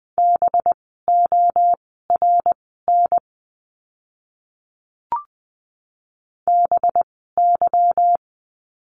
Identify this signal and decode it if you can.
Morse code, heard as a keyed tone, about 20 wpm, spaced as for 14 wpm, 700 Hz. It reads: BORN BY
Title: 185 Lesson 30f - Letter b - Words 20wpm